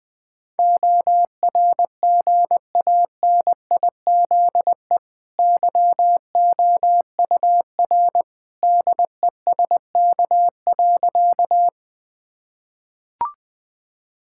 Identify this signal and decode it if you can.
Morse code, heard as a keyed tone, about 20 wpm, 700 Hz. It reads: ORGANIZE YOUR DESK.
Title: Easy Sentences with Hints - 20wpm